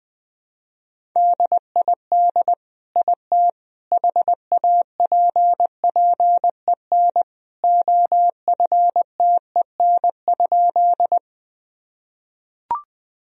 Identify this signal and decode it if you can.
Morse code, heard as a keyed tone, about 21 wpm, 700 Hz. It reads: DID IT HAPPEN OFTEN?
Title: Sentences from Top 1000 Words - Review Part 1 - 20wpm